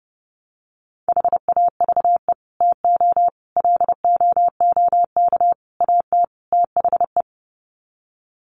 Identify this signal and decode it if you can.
Morse code, heard as a keyed tone, about 30 wpm, 700 Hz. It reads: HAVE TO LOOK AT THE
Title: Sets of 5 Words Encore 30wpm